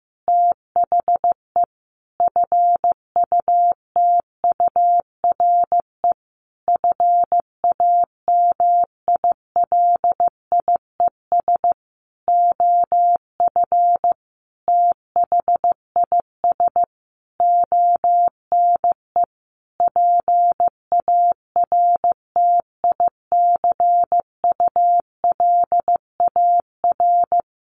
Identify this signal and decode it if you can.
Morse code, heard as a keyed tone, about 15 wpm, 700 Hz. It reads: THE FUTURE FAMILIES OF THIS ONE PARTICULAR
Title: A Sound Of Thunder 15wpm